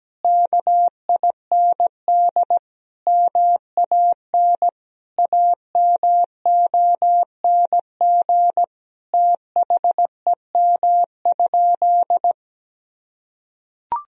Morse code, heard as a keyed tone, about 17 wpm, 700 Hz. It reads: KIND MAN AMONG THEM?
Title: Sentences from Top 1000 Words - Review Part 3 - 17wpm